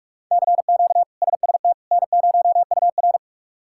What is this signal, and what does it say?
Morse code, heard as a keyed tone, about 46 wpm, 700 Hz. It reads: CQ SST N0UR
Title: K1USN SST QSO Practice - 45wpm